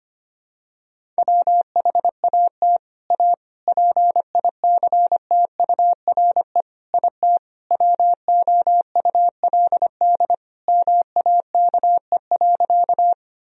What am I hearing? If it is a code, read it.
Morse code, heard as a keyed tone, about 25 wpm, 700 Hz. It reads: WHAT A PICTURE IT WOULD MAKE.
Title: Sentences from Top 1000 Words - Review Part 1 - 25wpm